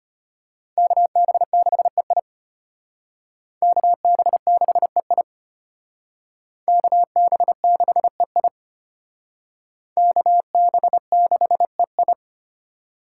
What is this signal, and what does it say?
Morse code, heard as a keyed tone, about 33 wpm, 700 Hz. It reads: KB6EI KB6EI KB6EI KB6EI